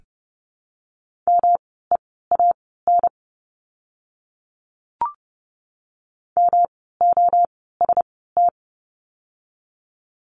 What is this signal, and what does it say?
Morse code, heard as a keyed tone, about 30 wpm, spaced as for 17 wpm, 700 Hz. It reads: MEAN MOST
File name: course / 30:10wpm / 237 Lesson 38f - Letter q - Words 30wpm